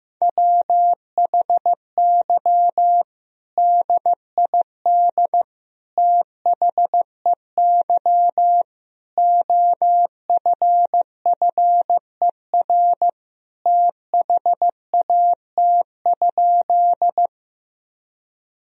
Morse code, heard as a keyed tone, about 15 wpm, 700 Hz. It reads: WHY DID THEY OFFER THAT?
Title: Sentences from Top 1000 Words - Review Part 4 - 15wpm